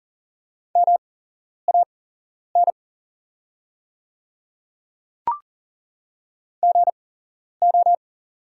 Morse code, heard as a keyed tone, about 40 wpm, spaced as for 11 wpm, 700 Hz. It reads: MAN GO
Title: 8x Character Spacing - Top 200 Words - 40wpm